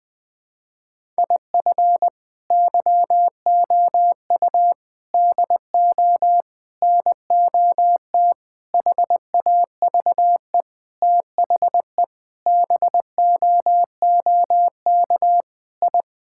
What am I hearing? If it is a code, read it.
Morse code, heard as a keyed tone, about 21 wpm, 700 Hz. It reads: IF YOU DO NOT HAVE THE BOOK I
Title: Sentences with 5-Word Phrases - 20wpm